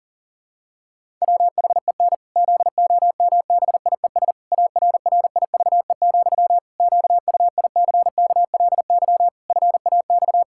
Morse code, heard as a keyed tone, about 40 wpm, 700 Hz. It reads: WHEN ZOMBIES ARRIVE, QUICKLY FAX
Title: Pangram Challenge v2 - 40wpm